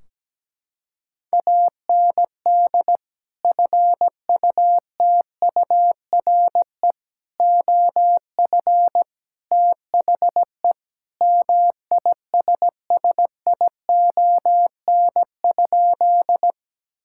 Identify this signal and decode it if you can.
Morse code, heard as a keyed tone, about 17 wpm, 700 Hz. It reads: AND FUTURE OF THE MISSION?
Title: Sentences from Top 1600 Words 17wpm